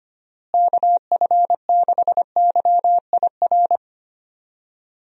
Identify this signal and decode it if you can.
Morse code, heard as a keyed tone, about 25 wpm, 700 Hz. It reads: KF6YIR
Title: USA Call Signs - 6 Characters - Repeated 2x - 25wpm